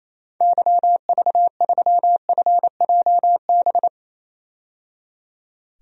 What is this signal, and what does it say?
Morse code, heard as a keyed tone, about 27 wpm, 700 Hz. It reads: YV3FJB
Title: DXCC-100 Call Signs - 6 Characters - 28wpm